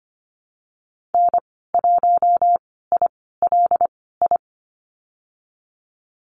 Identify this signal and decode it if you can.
Morse code, heard as a keyed tone, about 25 wpm, spaced as for 16 wpm, 700 Hz. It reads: N1ILI
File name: Lesson 15g - Number 5 - Call Signs 25wpm